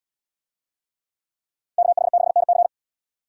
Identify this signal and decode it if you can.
Morse code, heard as a keyed tone, about 77 wpm, 700 Hz. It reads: KH6NC